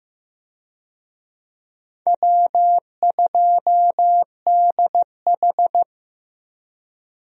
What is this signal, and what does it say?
Morse code, heard as a keyed tone, about 15 wpm, 700 Hz. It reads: W2DH
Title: USA Call Signs - 4 Characters - Repeated 3x - 15wpm